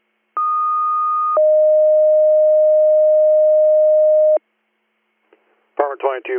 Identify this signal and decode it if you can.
Two-tone page, heard as a one-second tone, then a three-second tone, about 1230 then 620 Hz, with speech. It is heard from 0.4 to 4.4 s.